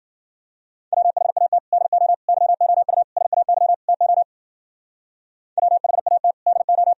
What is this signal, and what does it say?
Morse code, heard as a keyed tone, about 52 wpm, 700 Hz. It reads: WHAT DO YOU SAY TO WHAT DO